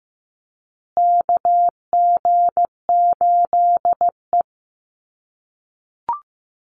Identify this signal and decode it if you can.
Morse code, heard as a keyed tone, about 15 wpm, 700 Hz. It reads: KG8E
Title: USA Call Signs Repeated 3x 15wpm